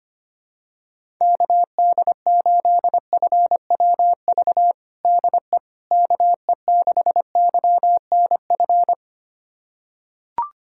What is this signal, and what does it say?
Morse code, heard as a keyed tone, about 25 wpm, 700 Hz. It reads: KD8FWV DE KE6YNF
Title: USA Callsign DE Callsign 25wpm